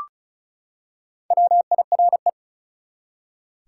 Morse code, heard as a keyed tone, about 36 wpm, 700 Hz. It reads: WIRE
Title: Top 180 QSO Elements 35wpm